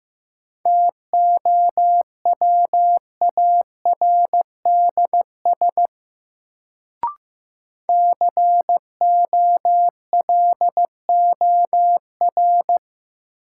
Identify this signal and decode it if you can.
Morse code, heard as a keyed tone, about 16 wpm, 700 Hz. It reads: TOWARDS COLOR
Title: Sentences from Top 1000 Words - Review Part 2 - New Word Drill - 15wpm